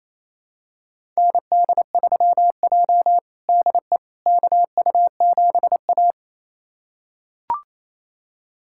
Morse code, heard as a keyed tone, about 27 wpm, 700 Hz. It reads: ND3J DE KU7A
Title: USA Callsign DE Callsign - 4 Characters - 28wpm